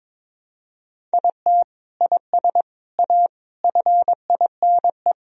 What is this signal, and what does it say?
Morse code, heard as a keyed tone, about 22 wpm, 700 Hz. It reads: IT IS A FINE
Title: Sentences from Top 1000 Words - Review Part 3 - 22wpm